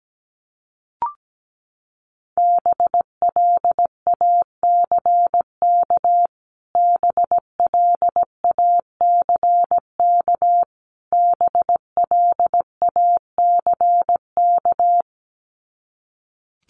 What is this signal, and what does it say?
Morse code, heard as a keyed tone, about 17 wpm, 700 Hz. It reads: BLACK BLACK BLACK
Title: Top 500 Words Repeated 17wpm